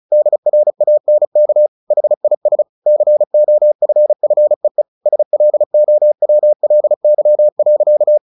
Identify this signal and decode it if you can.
Morse code, heard as a keyed tone, about 34 wpm, 600 Hz. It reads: DRANK HIS COFFEE SLOWLY.
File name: the old man and the sea / the old man and the sea 1 35wpm